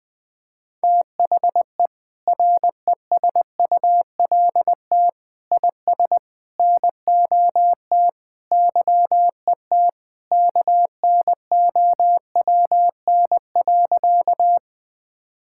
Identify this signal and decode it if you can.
Morse code, heard as a keyed tone, about 20 wpm, 700 Hz. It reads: THE RESULT IS NOT YET KNOWN.
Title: Sentences from Top 1000 Words - Review Part 5 - 20wpm